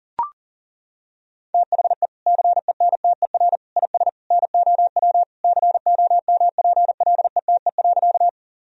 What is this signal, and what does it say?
Morse code, heard as a keyed tone, about 40 wpm, 700 Hz. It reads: THE CENTER IS NOW COMPLETE.
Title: Sentences from Top 1000 Words - Review Part 2 - 40wpm